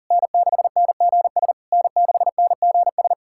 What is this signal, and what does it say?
Morse code, heard as a keyed tone, about 40 wpm, 700 Hz. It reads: N6NGS N6NGS
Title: SOTA QSO Practice 40wpm